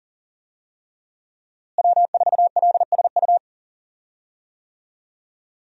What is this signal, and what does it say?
Morse code, heard as a keyed tone, about 37 wpm, 700 Hz. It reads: W4LSU
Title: USA Call Signs - 5 Characters - Repeated 3x - 40wpm